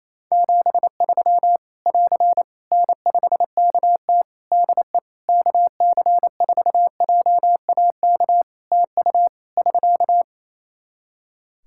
Morse code, heard as a keyed tone, about 28 wpm, 700 Hz. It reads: 73 <AR> N5KT DE KC4JAK TU <SK>
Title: K1USN SST QSO Practice 28wpm